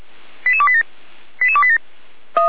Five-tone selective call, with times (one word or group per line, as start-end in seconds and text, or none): 0.5-0.8 88287
1.4-1.8 88287